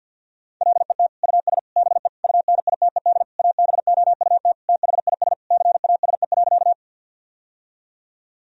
Morse code, heard as a keyed tone, about 50 wpm, 700 Hz. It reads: LET US BE UNITED ABOUT THIS CASE.
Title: Sentences from Top 1000 Words - Review Part 4 - 50wpm